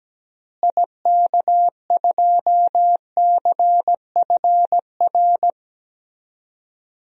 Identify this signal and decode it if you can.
Morse code, heard as a keyed tone, about 17 wpm, 700 Hz. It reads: IK2CFR
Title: DXCC-100 Call Signs - 6 Characters - Repeated 2x - 17wpm